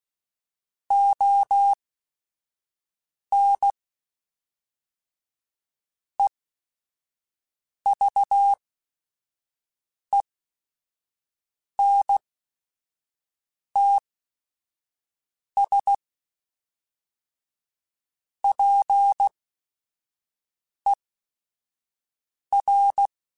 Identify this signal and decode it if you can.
Morse code, heard as a keyed tone, about 16 wpm, spaced as for 5 wpm, 800 Hz. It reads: ON EVENTS PER
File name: Technical Subjects / Technical Subjects-mp3s / Reactance @5